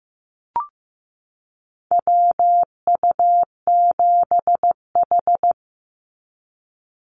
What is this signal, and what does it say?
Morse code, heard as a keyed tone, about 15 wpm, 700 Hz. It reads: WU7H
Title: NA SOTA Activator Callsigns 15wpm